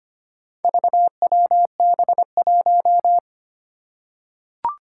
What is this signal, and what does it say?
Morse code, heard as a keyed tone, about 25 wpm, 700 Hz. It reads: VWB1
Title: Four Challenging Letter-Number Combinations 25wpm